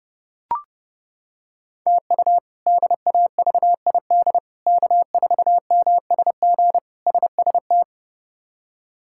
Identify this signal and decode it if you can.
Morse code, heard as a keyed tone, about 32 wpm, 700 Hz. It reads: TU DAVID K4MSG SST
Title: K1USN SST QSO Practice - 30wpm